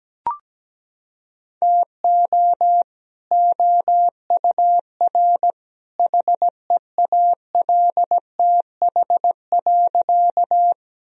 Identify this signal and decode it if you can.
Morse code, heard as a keyed tone, about 17 wpm, 700 Hz. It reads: TO OUR HEALTH.
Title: Sentences from Top 1000 Words - Review Part 3 - 17wpm